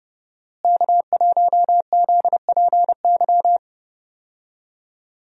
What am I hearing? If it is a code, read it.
Morse code, heard as a keyed tone, about 30 wpm, 700 Hz. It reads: K1ZPY
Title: USA Call Signs - 5 Characters - Repeated 3x - 30wpm